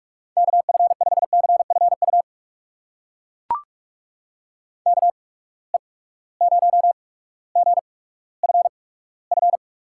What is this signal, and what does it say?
Morse code, heard as a keyed tone, about 42 wpm, spaced as for 19 wpm, 700 Hz. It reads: KF5CFU KE0GFF
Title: USA Call Signs - 6 Characters - 8x Character Spacing - 45wpm